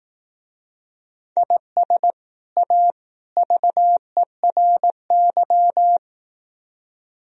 Morse code, heard as a keyed tone, about 18 wpm, 700 Hz. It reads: IS A VERY
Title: Speed-Racing - Sets of 3 Words 15wpm